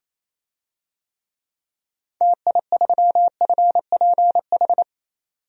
Morse code, heard as a keyed tone, about 27 wpm, 700 Hz. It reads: TI3FPH
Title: DXCC-100 Call Signs - 6 Characters - 28wpm